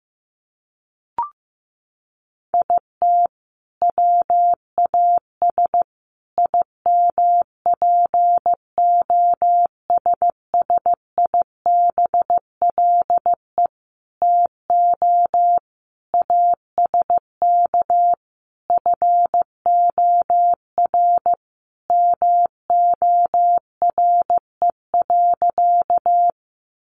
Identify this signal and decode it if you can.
Morse code, heard as a keyed tone, about 14 wpm, 700 Hz. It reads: IT WAS IMPOSSIBLE TO ASK FOR MORE.
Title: Sentences from Top 1100 Words 15wpm